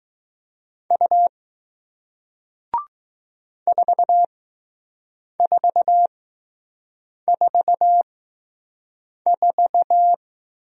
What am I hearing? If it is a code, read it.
Morse code, heard as a keyed tone, about 20 wpm, spaced as for 6 wpm, 700 Hz. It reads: U 4444